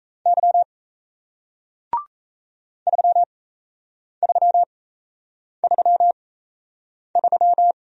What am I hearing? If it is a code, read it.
Morse code, heard as a keyed tone, about 38 wpm, spaced as for 8 wpm, 700 Hz. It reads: Y 3333